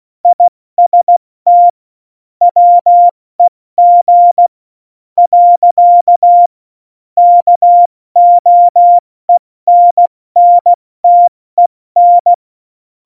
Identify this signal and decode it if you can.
Morse code, heard as a keyed tone, about 16 wpm, 700 Hz. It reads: IST WEG . KOENNTEN
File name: Morsekurs / Witze / Witz 13 March 2018, 70 F80